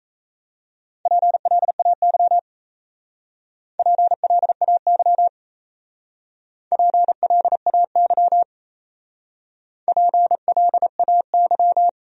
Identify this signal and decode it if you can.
Morse code, heard as a keyed tone, about 34 wpm, 700 Hz. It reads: PLAY PLAY PLAY PLAY
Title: Speed-Racing Top 200 Words 28wpm